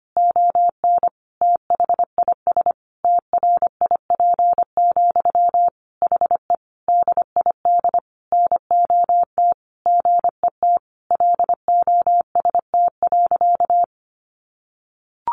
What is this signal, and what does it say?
Morse code, heard as a keyed tone, about 25 wpm, 700 Hz. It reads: ON THIS TRIP, HE DID NOT GET LOST.
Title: Sentences from Top 1400 Words 25wpm